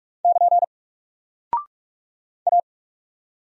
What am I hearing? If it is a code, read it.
Morse code, heard as a keyed tone, about 44 wpm, 700 Hz. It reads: <KN> A
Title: Single Character (Letter:Number:Punctuation:Prosign) Alt 45wpm